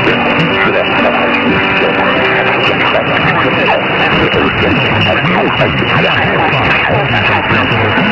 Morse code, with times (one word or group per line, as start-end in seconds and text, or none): none